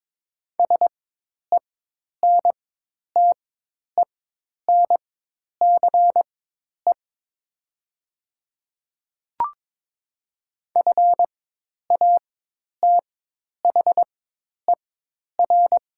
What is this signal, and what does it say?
Morse code, heard as a keyed tone, about 22 wpm, spaced as for 10 wpm, 690 Hz. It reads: SENTENCE FATHER
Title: 4x Character Spacing - Top 300 Words - 22wpm